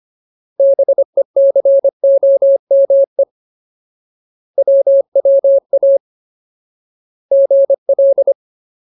Morse code, heard as a keyed tone, about 25 wpm, 550 Hz. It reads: BECOME WWA GL
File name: Ultimate Word List / 25WPM Ultimate